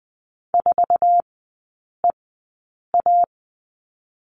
Morse code, heard as a keyed tone, about 20 wpm, 700 Hz. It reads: 4 E A
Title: 047 Lesson 9e - Number 4 - Three Characters 20wpm